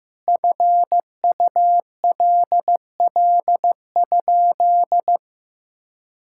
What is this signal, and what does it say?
Morse code, heard as a keyed tone, about 15 wpm, 700 Hz. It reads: FULL?
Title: Sentences from Top 1600 Words 15wpm